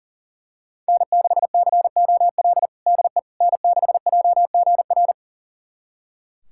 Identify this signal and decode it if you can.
Morse code, heard as a keyed tone, about 40 wpm, 700 Hz. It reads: N6COL DE N6JGR
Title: USA Callsign DE Callsign - 5 Characters - 40wpm